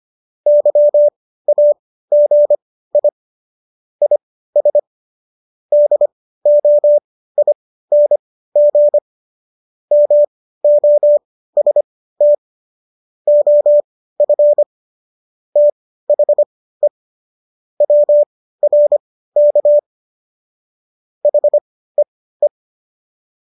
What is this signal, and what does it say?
Morse code, heard as a keyed tone, about 25 wpm, spaced as for 15 wpm, 590 Hz. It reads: YAGI IS DOING MOST OF THE WRK HEE